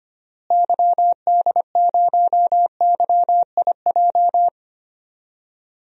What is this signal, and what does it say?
Morse code, heard as a keyed tone, about 25 wpm, 700 Hz. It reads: YD0YIJ
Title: DXCC-100 Call Signs - 6 Characters - Repeated 3x - 25wpm